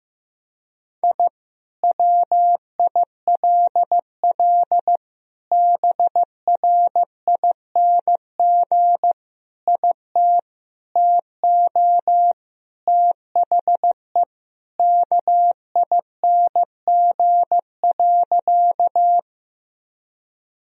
Morse code, heard as a keyed tone, about 15 wpm, 700 Hz. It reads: I WILL BRING IT TO THE KING.
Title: Sentences from Top 1000 Words - Review Part 3 - 15wpm